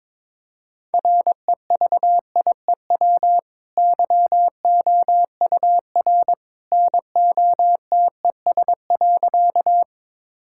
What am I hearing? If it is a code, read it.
Morse code, heard as a keyed tone, about 22 wpm, 700 Hz. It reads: REVIEW YOUR NOTES.
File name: practice / 22wpm / Easy Sentences with Hints - 22wpm